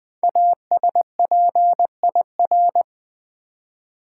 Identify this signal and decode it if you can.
Morse code, heard as a keyed tone, about 20 wpm, 700 Hz. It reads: ASPIR
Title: Build A Word - 5 Letters At A Time - 10-Letter Words - 20wpm